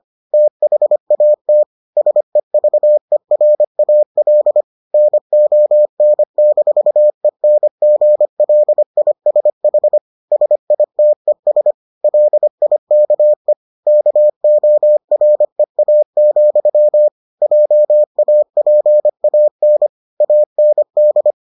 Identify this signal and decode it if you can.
Morse code, heard as a keyed tone, about 25 wpm, 600 Hz. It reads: THAT SEVERAL NON-ENGLISH SITES LIKE KOREA, JAPAN AND